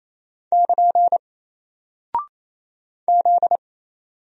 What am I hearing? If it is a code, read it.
Morse code, heard as a keyed tone, about 27 wpm, 700 Hz. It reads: <KN> Z
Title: Single Character (Letter:Number:Punctuation:Prosign) 28wpm